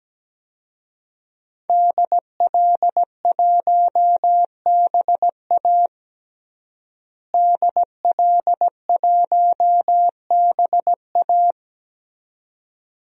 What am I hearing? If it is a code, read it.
Morse code, heard as a keyed tone, about 17 wpm, 700 Hz. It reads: DL1BA DL1BA
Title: DXCC-100 Call Signs - 5 Characters - Repeated 2x - 17wpm